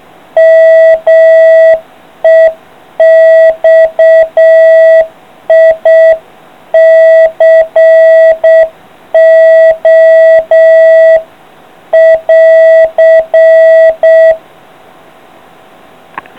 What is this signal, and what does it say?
Morse code, heard as a keyed tone, about 7 wpm, 660 Hz. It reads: MEXICO <AR>